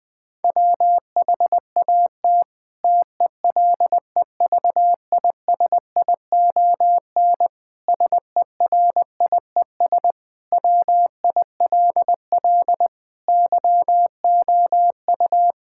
Morse code, heard as a keyed tone, about 21 wpm, 700 Hz. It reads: WHAT TELEVISION SERIES WILL YOU
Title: Sentences from Top 1000 Words - Review Part 5 - 20wpm